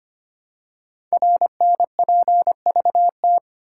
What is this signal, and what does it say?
Morse code, heard as a keyed tone, about 25 wpm, 700 Hz. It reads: RNPVT
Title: Code Groups - 5 Letters - 25wpm